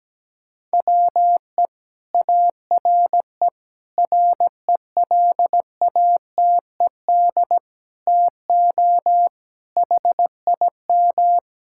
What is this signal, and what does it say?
Morse code, heard as a keyed tone, about 17 wpm, 700 Hz. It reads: WE ARE RELATED TO HIM
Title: Sentences from Top 1000 Words - Review Part 5 - 17wpm